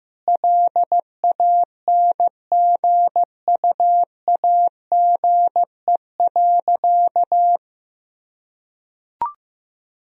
Morse code, heard as a keyed tone, about 15 wpm, 700 Hz. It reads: LANGUAGE.
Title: Sentences from Top 1000 Words - Review Part 3 - 15wpm